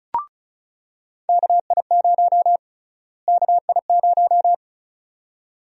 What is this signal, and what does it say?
Morse code, heard as a keyed tone, about 36 wpm, 700 Hz. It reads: KI0 KI0
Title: USA Call Sign Prefixes - Repeated 2x - 35wpm